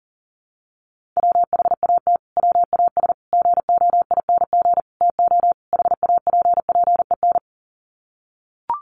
Morse code, heard as a keyed tone, about 40 wpm, 700 Hz. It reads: WHAT WAS GOING TO HAPPEN
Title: Sets of 5 Words Encore 40wpm